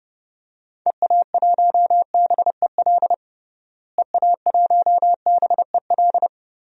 Morse code, heard as a keyed tone, about 32 wpm, 700 Hz. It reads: EA1BEL EA1BEL
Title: DXCC-100 Call Signs - 6 Characters - Repeated 2x - 30wpm